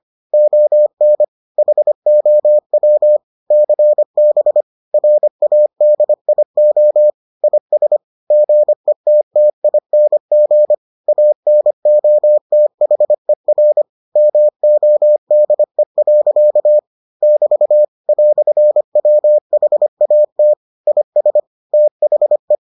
Morse code, heard as a keyed tone, about 25 wpm, 600 Hz. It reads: ON HOW CB RADIO IS GETTING ANOTHER MODE. <BT> "WHAT IS THE